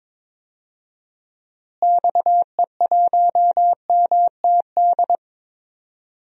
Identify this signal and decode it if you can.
Morse code, heard as a keyed tone, about 21 wpm, 700 Hz. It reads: XE1MTD
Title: DXCC-100 Call Signs - 6 Characters - Repeated 2x - 22wpm